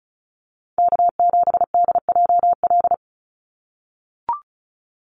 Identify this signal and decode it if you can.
Morse code, heard as a keyed tone, about 34 wpm, 700 Hz. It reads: K7DJL
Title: NA SOTA Chaser Callsigns 35wpm